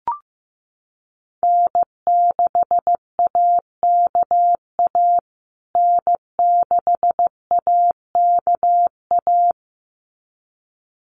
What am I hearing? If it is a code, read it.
Morse code, heard as a keyed tone, about 15 wpm, 700 Hz. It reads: N6AKA N6AKA
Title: USA Call Signs Repeated 2x 15wpm